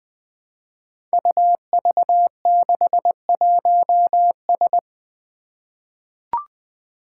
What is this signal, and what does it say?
Morse code, heard as a keyed tone, about 20 wpm, 700 Hz. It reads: UV61S
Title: Five Challenging Letter-Number Combinations 20wpm